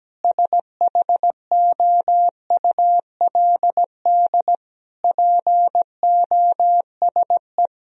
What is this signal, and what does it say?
Morse code, heard as a keyed tone, about 17 wpm, 700 Hz. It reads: SHOULD POSE
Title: Sentences from Top 1000 Words - Review Part 2 - 17wpm